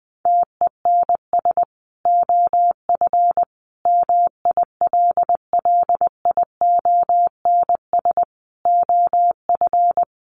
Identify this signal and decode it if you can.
Morse code, heard as a keyed tone, about 20 wpm, 700 Hz. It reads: TENS OF MILLIONS OF
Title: Sets of 5 Words Encore 20wpm